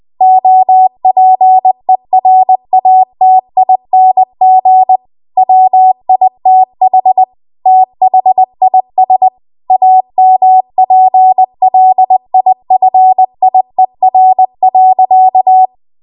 Morse code, heard as a keyed tone, about 20 wpm, 750 Hz. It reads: OPERATING WITH THIS AMPLIFIER.
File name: Archive / 20 WPM / 20WPM